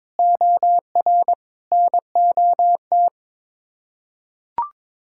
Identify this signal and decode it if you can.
Morse code, heard as a keyed tone, about 22 wpm, 700 Hz. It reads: OR NOT
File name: practice / 22wpm / Sets of 8 Words - 22wpm